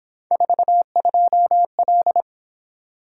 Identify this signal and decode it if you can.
Morse code, heard as a keyed tone, about 26 wpm, 700 Hz. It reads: 42L
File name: Speed-Racing - Three Letter-Number Combinations - 22wpm